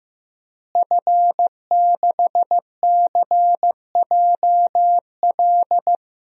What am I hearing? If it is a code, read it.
Morse code, heard as a keyed tone, about 16 wpm, 700 Hz. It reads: F6CJL
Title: DXCC-100 Call Signs - 5 Characters - Repeated 3x - 15wpm